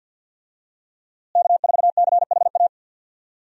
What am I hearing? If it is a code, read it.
Morse code, heard as a keyed tone, about 51 wpm, 700 Hz. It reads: K4CHA
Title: USA Call Signs - 5 Characters - Repeated 2x - 50wpm